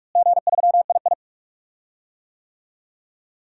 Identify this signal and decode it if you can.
Morse code, heard as a keyed tone, about 46 wpm, 700 Hz. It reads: G3II
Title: DXCC-100 Call Signs - 4 Characters - Repeated 3x - 45wpm